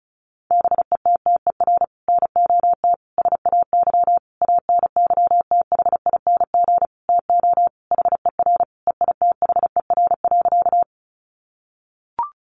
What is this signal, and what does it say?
Morse code, heard as a keyed tone, about 34 wpm, 700 Hz. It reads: BETTER NOT SAY ANYTHING TO HER EITHER.
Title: Sentences from Top 1000 Words 35wpm